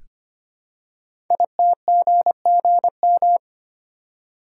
Morse code, heard as a keyed tone, about 25 wpm, 700 Hz. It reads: ITGGM
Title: Code Groups - 5 Letters - 25wpm